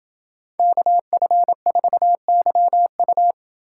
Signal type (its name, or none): Morse code